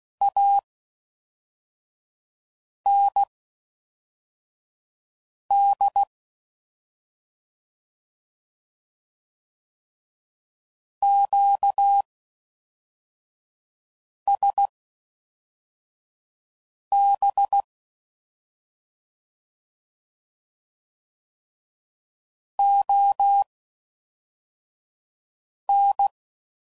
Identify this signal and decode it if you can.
Morse code, heard as a keyed tone, about 16 wpm, spaced as for 4 wpm, 800 Hz. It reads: AND QSB ON